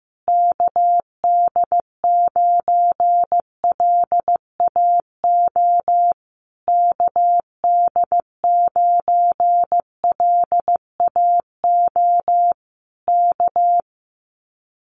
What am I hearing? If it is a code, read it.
Morse code, heard as a keyed tone, about 15 wpm, 700 Hz. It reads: KD9LAO KD9LAO K
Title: Common QSO Phrases Encore 15wpm